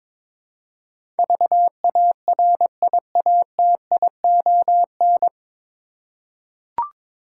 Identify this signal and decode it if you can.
Morse code, heard as a keyed tone, about 21 wpm, 700 Hz. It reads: VARIATION
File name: 9 Letter Words - First 3 Letters Given - 22wpm